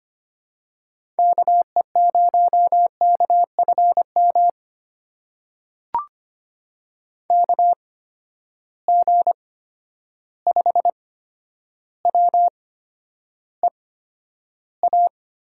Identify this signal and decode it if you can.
Morse code, heard as a keyed tone, about 25 wpm, spaced as for 11 wpm, 700 Hz. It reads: KE0KFM KG5WEA
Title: USA Call Signs - 6 Characters - 8x Character Spacing - 25wpm